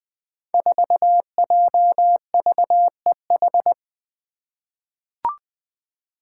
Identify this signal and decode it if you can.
Morse code, heard as a keyed tone, about 21 wpm, 700 Hz. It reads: 4JVEH
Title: Five Challenging Letter-Number Combinations 20wpm